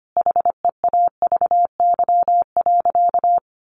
Morse code, heard as a keyed tone, about 25 wpm, 700 Hz. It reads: HEAVY.
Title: Sentences from Top 1100 Words 25wpm